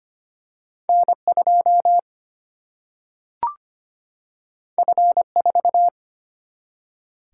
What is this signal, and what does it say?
Morse code, heard as a keyed tone, about 25 wpm, 700 Hz. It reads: N 2 F 4
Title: Single-Letter Single-Number 25wpm